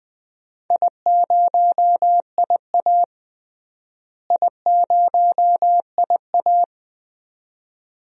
Morse code, heard as a keyed tone, about 20 wpm, 700 Hz. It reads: I0IA I0IA
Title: DXCC-100 Call Signs - 4 Characters - Repeated 2x - 20wpm